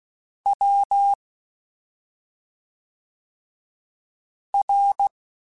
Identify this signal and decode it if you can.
Morse code, heard as a keyed tone, about 16 wpm, 800 Hz. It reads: W R
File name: Special INGs / Special INGs-mp3s / SlowToFaster ing @5